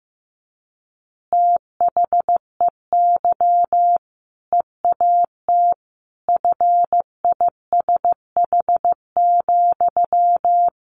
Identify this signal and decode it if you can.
Morse code, heard as a keyed tone, about 15 wpm, 700 Hz. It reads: THEY EAT FISH,
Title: Easy Phrases 15wpm